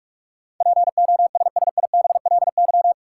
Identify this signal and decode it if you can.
Morse code, heard as a keyed tone, about 44 wpm, 700 Hz. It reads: POSSIBLY
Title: Sentences from Top 1300 Words - New Word Drill - 45wpm